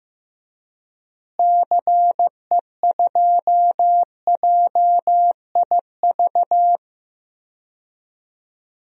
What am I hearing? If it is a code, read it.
Morse code, heard as a keyed tone, about 15 wpm, 700 Hz. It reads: CE2JIV
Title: DXCC-100 Call Signs - 6 Characters - Repeated 3x - 15wpm